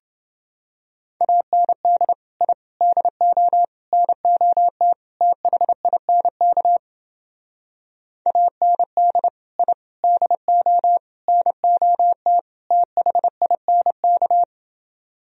Morse code, heard as a keyed tone, about 30 wpm, 700 Hz. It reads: AND I DO NOT THINK AND I DO NOT THINK